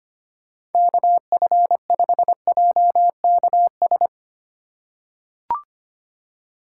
Morse code, heard as a keyed tone, about 25 wpm, 700 Hz. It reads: KF5JKS
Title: USA Call Signs - 6 Characters - 4x Character Spacing - 25wpm